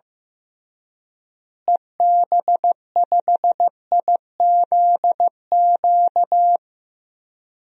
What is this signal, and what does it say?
Morse code, heard as a keyed tone, about 15 wpm, 700 Hz. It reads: EB5IZQ
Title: DXCC-100 Call Signs - 6 Characters - Repeated 3x - 15wpm